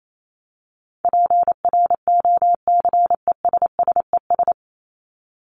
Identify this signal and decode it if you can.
Morse code, heard as a keyed tone, about 28 wpm, 700 Hz. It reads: PROCESSES
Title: ES:S Suffix - Top 100 Words with Base Words 28wpm